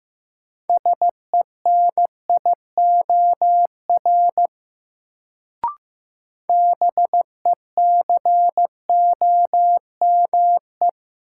Morse code, heard as a keyed tone, about 15 wpm, 700 Hz. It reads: SENIOR BECOME
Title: Sets of 5 Words Encore - New Word Drill - 15wpm